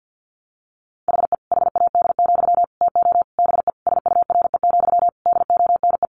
Morse code, heard as a keyed tone, about 50 wpm, 700 Hz. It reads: HE HAD, TO BE SURE, DONE
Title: Sentences from Top 500 Words 50wpm